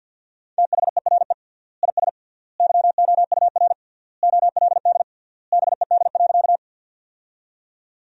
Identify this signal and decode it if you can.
Morse code, heard as a keyed tone, about 48 wpm, 700 Hz. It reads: THERE IS YOUR OLD BED.
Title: 3x Word-Spacing - Sentences from Top 500 Words - 50wpm